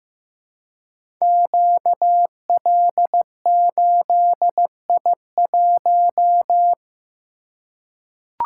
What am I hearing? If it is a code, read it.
Morse code, heard as a keyed tone, about 15 wpm, 700 Hz. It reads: QL8I1
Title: Code Groups - 5 Letter:Numbers - 15wpm